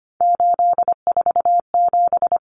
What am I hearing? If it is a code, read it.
Morse code, heard as a keyed tone, about 25 wpm, 700 Hz. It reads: 847
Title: Numbers - 3-Digit 25wpm